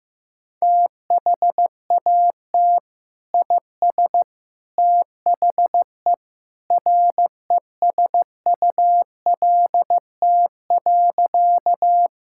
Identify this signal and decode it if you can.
Morse code, heard as a keyed tone, about 15 wpm, 700 Hz. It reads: THAT IS THE RESULT.
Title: Sentences from Top 1000 Words - Review Part 5 - 15wpm